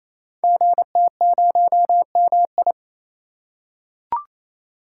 Morse code, heard as a keyed tone, about 28 wpm, 700 Hz. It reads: GT0MI
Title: Code Groups - 5 Letter:Numbers - 28wpm